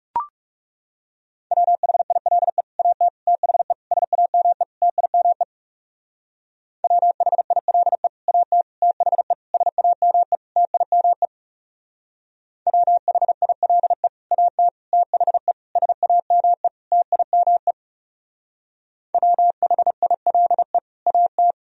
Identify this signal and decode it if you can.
Morse code, heard as a keyed tone, about 38 wpm, 700 Hz. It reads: WHILE AT THE SAME TIME WHILE AT THE SAME TIME WHILE AT THE SAME TIME WHILE AT